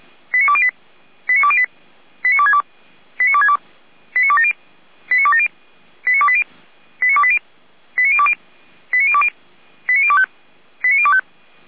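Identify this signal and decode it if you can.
Five-tone selective call (ZVEI), heard as quick runs of five tones, tones of about 70 ms. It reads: 89208 89208 89262 89262 89270 89270 89280 89280 89020 89020 89025 89025
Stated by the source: ZVEI five-tone selective call received by SDR.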